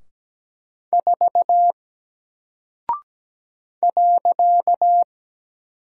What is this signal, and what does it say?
Morse code, heard as a keyed tone, about 17 wpm, 700 Hz. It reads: 4 .
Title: Single Character (Letter:Number:Punctuation:Prosign) 17wpm